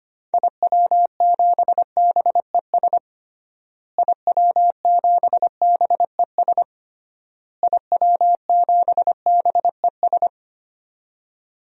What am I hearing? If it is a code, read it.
Morse code, heard as a keyed tone, about 25 wpm, 700 Hz. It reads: IW7BES IW7BES IW7BES
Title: DXCC-100 Call Signs - 6 Characters - Repeated 3x - 25wpm